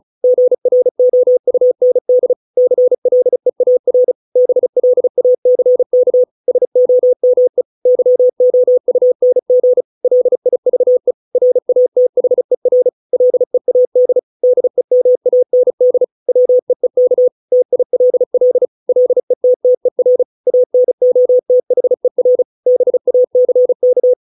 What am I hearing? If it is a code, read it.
Morse code, heard as a keyed tone, about 36 wpm, 500 Hz. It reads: GROUND CLEAR BLACK SOME YOUNG LIVE RATHER LEAD DEMAND WEEK TILL LETTER ANOTHER BACK